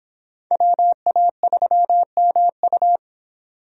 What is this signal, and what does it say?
Morse code, heard as a keyed tone, about 26 wpm, 700 Hz. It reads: WA3MU